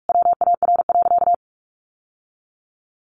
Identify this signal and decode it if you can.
Morse code, heard as a keyed tone, about 44 wpm, 700 Hz. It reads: WAR.
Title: Sentences from Top 1300 Words 45wpm